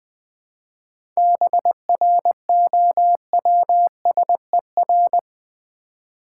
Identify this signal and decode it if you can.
Morse code, heard as a keyed tone, about 20 wpm, 700 Hz. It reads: BROWSER
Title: 7 Letter Words - First 3 Letters Given - 20wpm